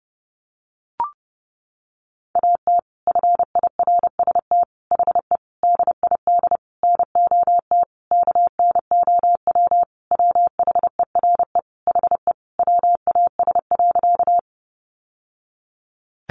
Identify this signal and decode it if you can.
Morse code, heard as a keyed tone, about 30 wpm, 700 Hz. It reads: AT FIRST HE DID NOT KNOW WHERE HE WAS.
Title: Sentences from Top 600 Words 30wpm